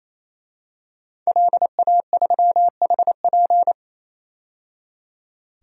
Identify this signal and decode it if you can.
Morse code, heard as a keyed tone, about 28 wpm, 700 Hz. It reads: LA3HP
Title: DXCC-100 Call Signs - 5 Characters - 28wpm